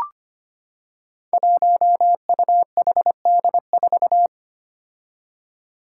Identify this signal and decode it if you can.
Morse code, heard as a keyed tone, about 25 wpm, 700 Hz. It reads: 1UHD4
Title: Five Challenging Letter-Number Combinations 25wpm